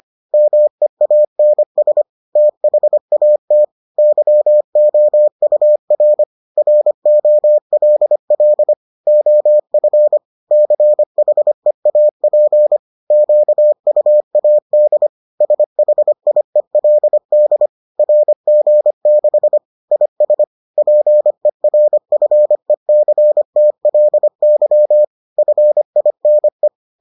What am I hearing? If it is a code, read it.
Morse code, heard as a keyed tone, about 25 wpm, 600 Hz. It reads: MEANS THAT YOUR ROLL OF CHEAP QUAD SHIELD RG6 IS PERFECTLY FINE